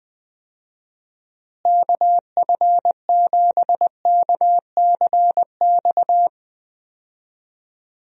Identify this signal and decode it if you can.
Morse code, heard as a keyed tone, about 20 wpm, 700 Hz. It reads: KF7KCX
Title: USA Call Signs - 6 Characters - Repeated 3x - 20wpm